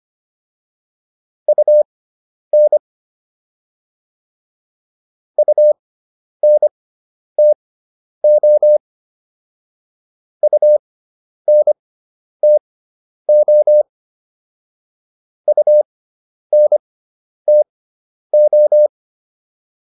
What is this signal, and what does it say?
Morse code, heard as a keyed tone, about 25 wpm, spaced as for 10 wpm, 600 Hz. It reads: UN UNTO UNTO UNTO